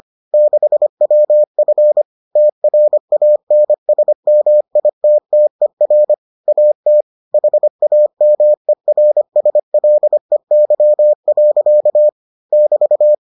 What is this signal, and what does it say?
Morse code, heard as a keyed tone, about 25 wpm, 600 Hz. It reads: 6WF TRANSMITTER AT HAMERSLEY. <BT>